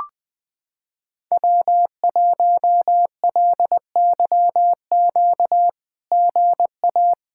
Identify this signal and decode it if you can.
Morse code, heard as a keyed tone, about 20 wpm, 700 Hz. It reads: W1LYQ GA
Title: SOTA QSO Practice 20wpm